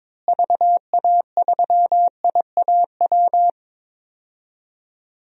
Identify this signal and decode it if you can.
Morse code, heard as a keyed tone, about 22 wpm, 700 Hz. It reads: VA3IAW
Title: DXCC-100 Call Signs - 6 Characters - Repeated 3x - 22wpm